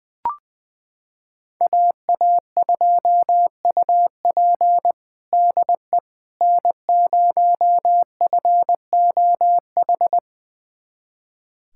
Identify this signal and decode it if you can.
Morse code, heard as a keyed tone, about 20 wpm, 700 Hz. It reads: AA2UP DE N0FOH
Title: USA Callsign DE Callsign - 5 Characters - 20wpm